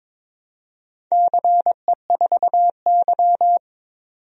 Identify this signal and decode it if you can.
Morse code, heard as a keyed tone, about 22 wpm, 690 Hz. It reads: CE4Y
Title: DXCC-100 Call Signs - 4 Characters - 22wpm